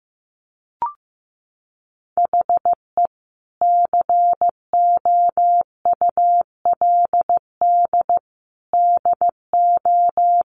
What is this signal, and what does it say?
Morse code, heard as a keyed tone, about 14 wpm, 700 Hz. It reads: HE COULD DO
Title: Sentences from Top 1500 Words 15wpm